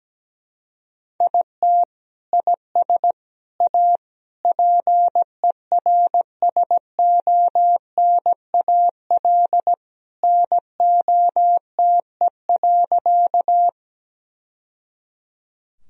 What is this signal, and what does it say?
Morse code, heard as a keyed tone, about 17 wpm, 700 Hz. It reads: IT IS A PERSONAL NOTE.
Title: Sentences from Top 1000 Words - Review Part 4 - 17wpm